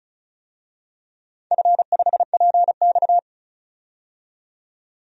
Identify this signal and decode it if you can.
Morse code, heard as a keyed tone, about 36 wpm, 700 Hz. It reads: F5PX
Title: DXCC-100 Call Signs - 4 Characters - 35wpm